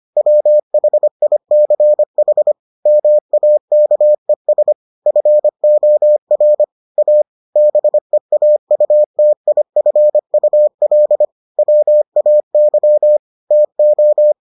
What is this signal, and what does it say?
Morse code, heard as a keyed tone, about 25 wpm, 600 Hz. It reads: WHICH MAKES FOR A BEAUTIFUL WAY TO